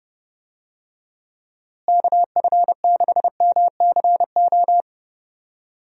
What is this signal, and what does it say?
Morse code, heard as a keyed tone, about 30 wpm, 700 Hz. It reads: KF6MCO
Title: USA Call Signs - 6 Characters - Repeated 3x - 30wpm